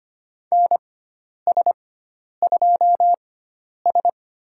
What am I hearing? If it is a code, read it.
Morse code, heard as a keyed tone, about 25 wpm, spaced as for 10 wpm, 700 Hz. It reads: NS2S
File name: Lesson 16g - Letter c - Call Signs 25wpm